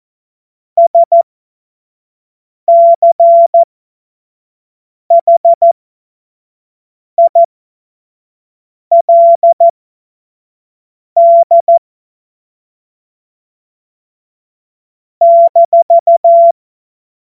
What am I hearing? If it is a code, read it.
Morse code, heard as a keyed tone, about 14 wpm, spaced as for 5 wpm, 680 Hz. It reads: SCHILD -
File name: Witz-21-Oktober-2025-25-F70